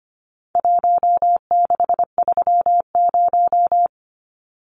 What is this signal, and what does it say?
Morse code, heard as a keyed tone, about 25 wpm, 700 Hz. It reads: 1630
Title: Numbers - Contest Style - 4-Digit 25wpm